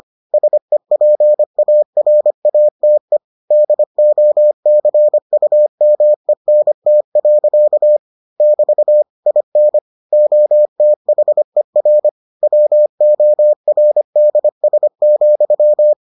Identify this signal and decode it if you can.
Morse code, heard as a keyed tone, about 25 wpm, 600 Hz. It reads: SEPARATE DOCUMENT. <BT> IN OTHER WORDS,